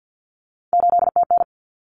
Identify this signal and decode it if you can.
Morse code, heard as a keyed tone, about 48 wpm, 700 Hz. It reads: 8TN
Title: Common Cut Numbers - 3-Digit 50wpm